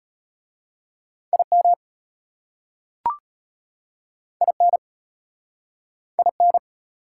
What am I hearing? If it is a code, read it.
Morse code, heard as a keyed tone, about 33 wpm, 700 Hz. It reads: IM IN IN